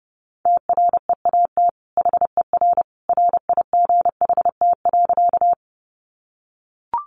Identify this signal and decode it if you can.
Morse code, heard as a keyed tone, about 30 wpm, 700 Hz. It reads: TREAT HER RIGHT.
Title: Sentences from Top 1300 Words 30wpm